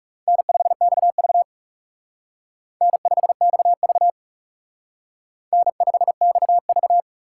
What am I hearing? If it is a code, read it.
Morse code, heard as a keyed tone, about 38 wpm, 700 Hz. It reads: N5XV N5XV N5XV